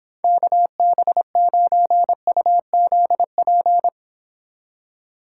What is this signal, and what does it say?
Morse code, heard as a keyed tone, about 26 wpm, 700 Hz. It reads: KB9UZP
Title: Speed Racing - USA 2x3 Call Signs 22wpm